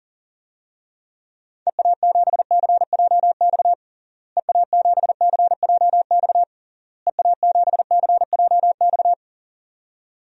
Morse code, heard as a keyed tone, about 40 wpm, 700 Hz. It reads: EA7CJX EA7CJX EA7CJX
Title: DXCC-100 Call Signs - 6 Characters - Repeated 3x - 40wpm